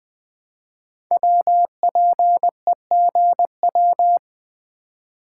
Morse code, heard as a keyed tone, about 20 wpm, 700 Hz. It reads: WPEGW
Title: Code Groups - 5 Letter:Numbers - 20wpm